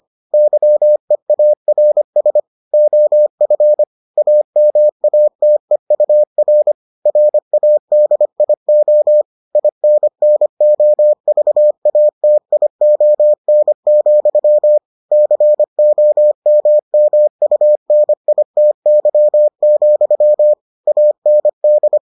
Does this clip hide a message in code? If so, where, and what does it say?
Yes: Morse code from 0.3 to 22.0 s, heard as a keyed tone, about 25 wpm, 600 Hz. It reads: YEARS OF AMATEUR RADIO INNOVATION, COMMUNITY, AND